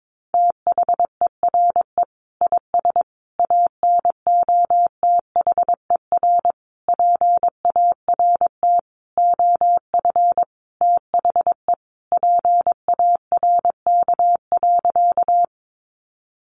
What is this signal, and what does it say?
Morse code, heard as a keyed tone, about 22 wpm, 700 Hz. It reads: THERE IS ANOTHER PART OF THE PARK.
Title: Sentences from Top 1300 Words 22wpm